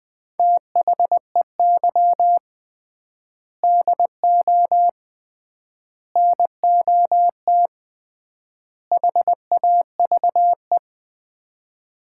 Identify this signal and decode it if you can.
Morse code, heard as a keyed tone, about 20 wpm, 700 Hz. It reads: THEY DO NOT HAVE
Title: 3x Word-Spacing - Sets of 4 Words - 20wpm